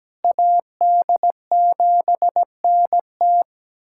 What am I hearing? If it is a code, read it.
Morse code, heard as a keyed tone, about 17 wpm, 700 Hz. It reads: AD7NT
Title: USA Callsign DE Callsign 17wpm